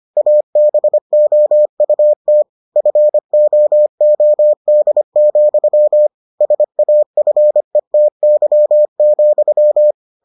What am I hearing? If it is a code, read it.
Morse code, heard as a keyed tone, about 25 wpm, 600 Hz. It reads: ABOUT FOOD, SAFETY,